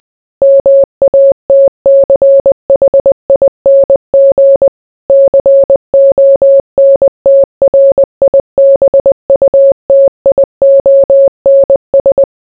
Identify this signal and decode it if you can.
Morse code, heard as a keyed tone, about 20 wpm, 550 Hz. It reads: MATCHING CONTRIBUTIONS